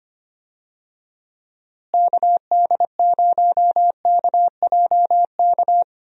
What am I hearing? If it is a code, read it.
Morse code, heard as a keyed tone, about 25 wpm, 700 Hz. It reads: KD0KJK
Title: USA Call Signs - 6 Characters - Repeated 3x - 25wpm